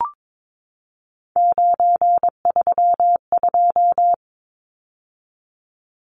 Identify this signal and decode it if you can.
Morse code, heard as a keyed tone, about 22 wpm, 700 Hz. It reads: 932
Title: Numbers - 3-Digit 22wpm